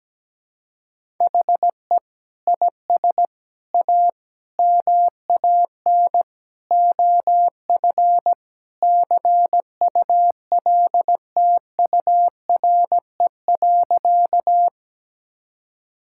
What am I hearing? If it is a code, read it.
Morse code, heard as a keyed tone, about 17 wpm, 700 Hz. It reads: HE IS A MAN OF CULTURE.
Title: Sentences from Top 1000 Words - Review Part 6 - 17wpm